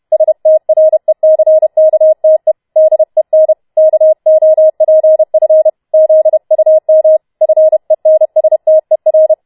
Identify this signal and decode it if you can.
Morse code, heard as a keyed tone, about 31 wpm, 630 Hz. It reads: STRECKTE DEN KOPF ZUM FENSTER